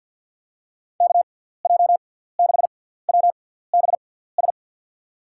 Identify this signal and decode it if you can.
Morse code, heard as a keyed tone, about 51 wpm, spaced as for 17 wpm, 700 Hz. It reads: KJ6WBS
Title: USA Call Signs - 6 Characters - 6x Character Spacing - 50wpm